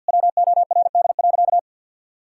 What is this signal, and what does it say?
Morse code, heard as a keyed tone, about 46 wpm, 700 Hz. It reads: WORD.
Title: Pangram Challenge v2 - 50wpm